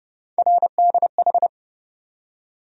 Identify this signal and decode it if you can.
Morse code, heard as a keyed tone, about 30 wpm, 700 Hz. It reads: RDH
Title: USA Call Sign Postfixes - 3 Letters 30wpm